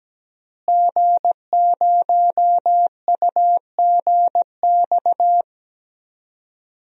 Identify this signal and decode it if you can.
Morse code, heard as a keyed tone, about 17 wpm, 700 Hz. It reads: G0UGX
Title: DXCC-100 Call Signs - 5 Characters - Repeated 2x - 17wpm